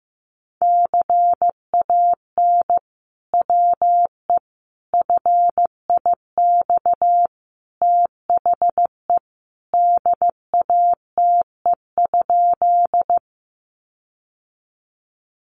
Morse code, heard as a keyed tone, about 15 wpm, 700 Hz. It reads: CAN WE FIX THE DATE?
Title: Sentences from Top 1400 Words 15wpm